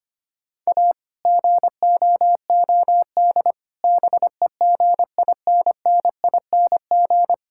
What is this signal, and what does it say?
Morse code, heard as a keyed tone, about 25 wpm, 700 Hz. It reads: A GOOD BEGINNING
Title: Sentences from Top 1000 Words - Review Part 5 - 25wpm